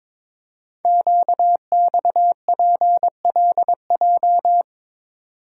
Morse code, heard as a keyed tone, about 22 wpm, 700 Hz. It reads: QXPLJ
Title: Code Groups - 5 Letters - 22wpm